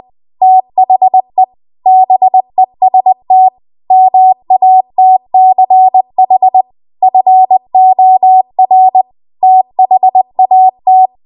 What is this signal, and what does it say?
Morse code, heard as a keyed tone, about 20 wpm, 750 Hz. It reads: THE BEST MATCH FOR THAT
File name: Archive / 20 WPM / 20WPM